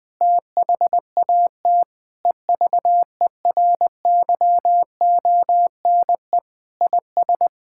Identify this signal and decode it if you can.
Morse code, heard as a keyed tone, about 21 wpm, 700 Hz. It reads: THAT EVERYONE IS
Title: Sentences with 7-Word Phrases - 20wpm